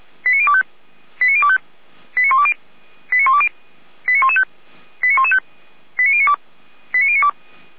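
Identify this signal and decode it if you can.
Five-tone selective call (ZVEI), heard as quick runs of five tones, tones of about 70 ms. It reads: 89025 89025 89120 89120 89115 89115 89092 89092
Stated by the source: ZVEI five-tone selective call received via SDR.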